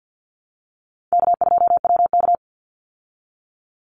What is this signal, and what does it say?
Morse code, heard as a keyed tone, about 50 wpm, 700 Hz. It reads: K2WK
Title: Easy Call Signs Repeated 2x - 50wpm